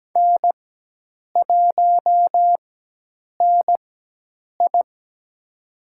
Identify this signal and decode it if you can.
Morse code, heard as a keyed tone, about 17 wpm, spaced as for 8 wpm, 700 Hz. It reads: N1NI
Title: USA Call Signs - 4x Character Spacing - 17wpm